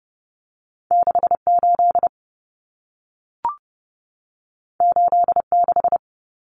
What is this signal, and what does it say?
Morse code, heard as a keyed tone, about 30 wpm, 700 Hz. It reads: 68 86
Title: Common Cut Numbers - 2-Digit 30wpm